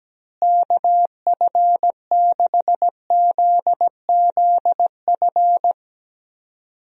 Morse code, heard as a keyed tone, about 17 wpm, 700 Hz. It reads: KF6ZZF
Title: USA Call Signs - 6 Characters - Repeated 3x - 17wpm